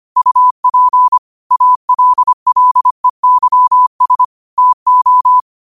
Morse code, heard as a keyed tone, about 25 wpm, 1000 Hz. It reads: UP ALLEYS TO